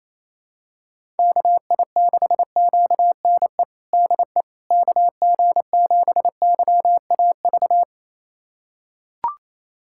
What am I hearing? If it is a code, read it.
Morse code, heard as a keyed tone, about 28 wpm, 700 Hz. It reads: KI6QNE DE KG7YAV
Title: USA Callsign DE Callsign 28wpm